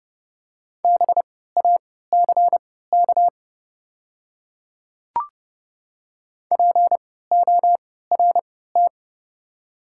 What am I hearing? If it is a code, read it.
Morse code, heard as a keyed tone, about 30 wpm, spaced as for 17 wpm, 700 Hz. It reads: BACK PORT
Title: 3x Character Spacing - Top 200 Words - 30wpm